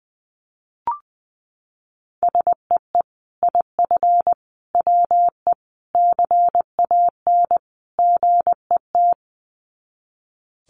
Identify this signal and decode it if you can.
Morse code, heard as a keyed tone, about 20 wpm, 700 Hz. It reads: SEE IF WE CAN GET
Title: Sets of 5 Words Encore2 20wpm